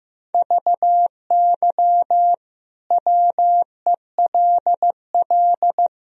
Morse code, heard as a keyed tone, about 15 wpm, 700 Hz. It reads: VY WELL
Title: Sentences from Top 200 Words and Abbreviations 15wpm